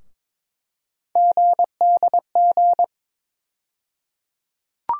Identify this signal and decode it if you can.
Morse code, heard as a keyed tone, about 22 wpm, 700 Hz. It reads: GDG
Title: USA Call Sign Postfixes - 3 Letters 22wpm